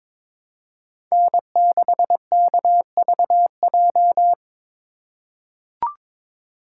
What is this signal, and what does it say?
Morse code, heard as a keyed tone, about 23 wpm, 700 Hz. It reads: N6KVJ
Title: USA Call Signs - 3x Character Spacing - 22wpm